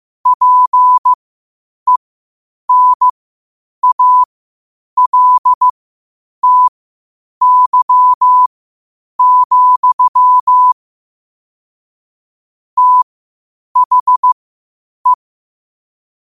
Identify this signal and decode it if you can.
Morse code, heard as a keyed tone, about 15 wpm, spaced as for 8 wpm, 1000 Hz. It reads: PENALTY, THE
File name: qotdfiles / 8wpm